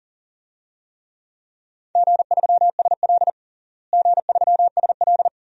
Morse code, heard as a keyed tone, about 40 wpm, 700 Hz. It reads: G3SL G3SL
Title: DXCC-100 Call Signs - 4 Characters - Repeated 2x - 40wpm